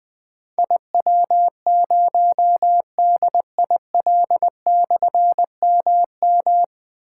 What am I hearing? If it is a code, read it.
Morse code, heard as a keyed tone, about 20 wpm, 700 Hz. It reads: IW0DIL/MM
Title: Difficult Call Signs 20wpm